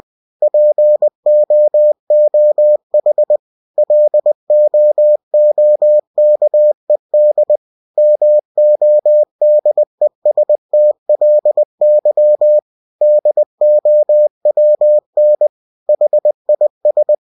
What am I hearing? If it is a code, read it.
Morse code, heard as a keyed tone, about 20 wpm, 600 Hz. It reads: POOH LOOKED MODESTLY DOWN HIS
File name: cw-books / winnie-the-pooh / winnie-20wpm-page-0099